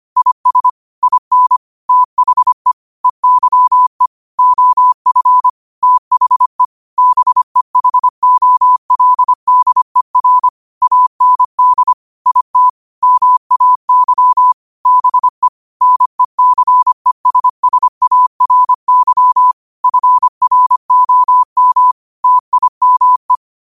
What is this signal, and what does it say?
Morse code, heard as a keyed tone, about 25 wpm, 1000 Hz. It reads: IS IN THE EYE OF THE BEHOLDER AND IT MAY BE NECESSARY FROM TIME